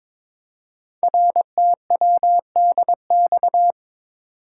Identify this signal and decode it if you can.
Morse code, heard as a keyed tone, about 22 wpm, 700 Hz. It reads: RTWDX
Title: Code Groups - 5 Letters - 22wpm